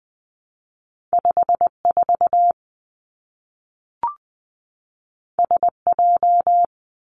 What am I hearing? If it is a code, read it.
Morse code, heard as a keyed tone, about 20 wpm, 700 Hz. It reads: 5 4 S J
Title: Two Challenging Letter-Number Combinations 20wpm